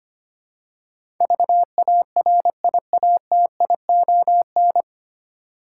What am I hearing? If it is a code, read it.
Morse code, heard as a keyed tone, about 25 wpm, 700 Hz. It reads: VARIATION
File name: practice / 25wpm / 9 Letter Words - First 3 Letters Given - 25wpm